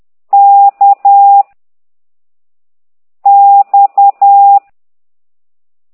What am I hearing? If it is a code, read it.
Morse code, heard as a keyed tone, about 10 wpm, 800 Hz. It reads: K X
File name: ct2idw / exerc9